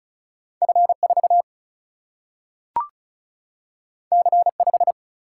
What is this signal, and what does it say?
Morse code, heard as a keyed tone, about 36 wpm, 700 Hz. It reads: F 4 C 5
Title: Single-Letter Single-Number 35wpm